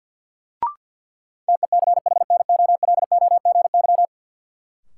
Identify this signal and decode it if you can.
Morse code, heard as a keyed tone, about 52 wpm, 700 Hz. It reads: TECHNOLOGY
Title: Sentences from Top 1000 Words - Review Part 5 - New Word Drill - 50wpm